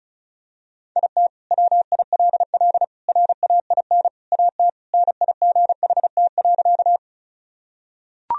Morse code, heard as a keyed tone, about 34 wpm, 700 Hz. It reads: IT WILL RAIN AT NIGHT.
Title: Sentences from Top 1000 Words - Review Part 1 - 35wpm